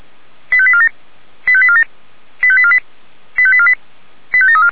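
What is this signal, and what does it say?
Five-tone selective call (ZVEI), heard as quick runs of five tones, tones of about 70 ms. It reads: 85847 85847 85848 85848 85734
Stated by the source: ZVEI five-tone call 85748 received by SDR, channel 464.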